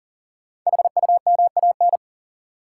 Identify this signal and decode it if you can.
Morse code, heard as a keyed tone, about 40 wpm, 700 Hz. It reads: HUMAN
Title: Sentences from Top 600 Words - New Word Drill - 40wpm